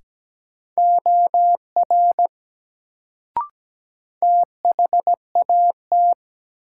Morse code, heard as a keyed tone, about 17 wpm, 700 Hz. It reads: OR THAT
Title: Sentences from Top 100 Words - New Word Drill - 17wpm